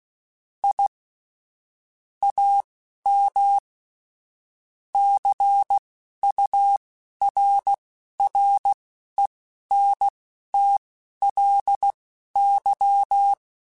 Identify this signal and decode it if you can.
Morse code, heard as a keyed tone, about 16 wpm, spaced as for 12 wpm, 800 Hz. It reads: I AM CURRENTLY
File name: QSOs / QSOs-mp3s / WD0FBT 1@10